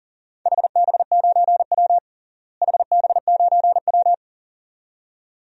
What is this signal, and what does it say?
Morse code, heard as a keyed tone, about 40 wpm, 700 Hz. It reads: HB9W HB9W
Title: DXCC-100 Call Signs - 4 Characters - Repeated 2x - 40wpm